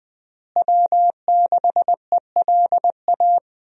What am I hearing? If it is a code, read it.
Morse code, heard as a keyed tone, about 21 wpm, 700 Hz. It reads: W6ELA
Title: USA Call Signs - 5 Characters - Repeated 3x - 20wpm